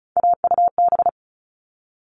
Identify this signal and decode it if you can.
Morse code, heard as a keyed tone, about 34 wpm, 700 Hz. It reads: AUB
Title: All Cut Numbers - 3-Digit 35wpm